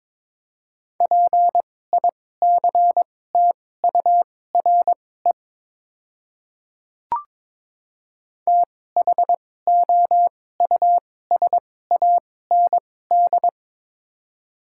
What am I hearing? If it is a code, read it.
Morse code, heard as a keyed tone, about 22 wpm, spaced as for 16 wpm, 700 Hz. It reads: PICTURE THOUSAND
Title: 2x Character Spacing - Top 500 Words - 22wpm